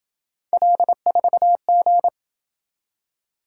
Morse code, heard as a keyed tone, about 27 wpm, 700 Hz. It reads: L4G